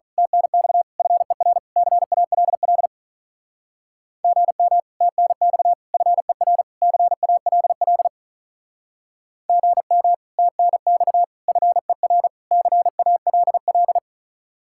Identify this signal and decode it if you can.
Morse code, heard as a keyed tone, about 42 wpm, 700 Hz. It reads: TNX FER CALL GM TNX FER CALL GM TNX FER CALL
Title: Speed-Racing Common QSO Phrases Encore - 35wpm